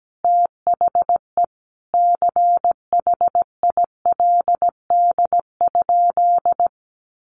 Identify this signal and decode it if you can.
Morse code, heard as a keyed tone, about 17 wpm, 700 Hz. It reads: THE CHILD?
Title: Sentences from Top 800 Words 17wpm